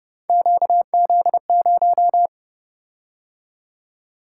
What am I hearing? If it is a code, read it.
Morse code, heard as a keyed tone, about 28 wpm, 700 Hz. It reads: QZ0